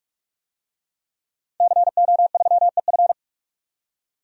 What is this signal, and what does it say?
Morse code, heard as a keyed tone, about 46 wpm, 700 Hz. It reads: CO3EF
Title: DXCC-100 Call Signs - 5 Characters - 45wpm